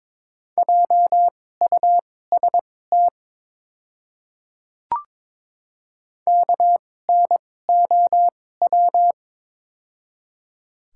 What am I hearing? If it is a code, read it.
Morse code, heard as a keyed tone, about 22 wpm, spaced as for 16 wpm, 700 Hz. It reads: JUST KNOW
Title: 2x Character Spacing - Top 200 Words - 22wpm